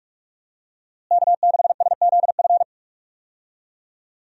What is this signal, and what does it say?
Morse code, heard as a keyed tone, about 46 wpm, 700 Hz. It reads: K6SZF
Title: USA Call Signs - 5 Characters - Repeated 3x - 45wpm